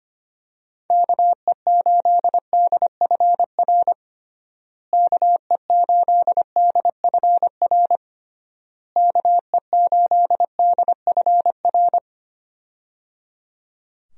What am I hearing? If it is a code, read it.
Morse code, heard as a keyed tone, about 25 wpm, 700 Hz. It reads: KE8DFR KE8DFR KE8DFR
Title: USA Call Signs - 6 Characters - Repeated 3x - 25wpm